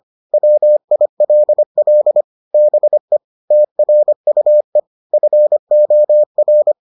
Morse code, heard as a keyed tone, about 25 wpm, 600 Hz. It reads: WILL BE TRUE FOR